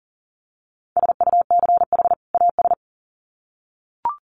Morse code, heard as a keyed tone, about 40 wpm, 700 Hz. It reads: SUCH AS
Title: Sets of 2 Words 40wpm